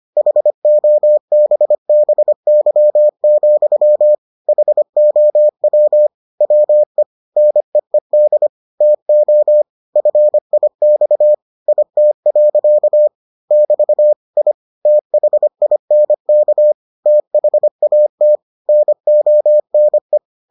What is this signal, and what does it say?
Morse code, heard as a keyed tone, about 25 wpm, 600 Hz. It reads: HOBBY, HOW WE NEED TO FIX IT. <BT> I THINK THAT NONE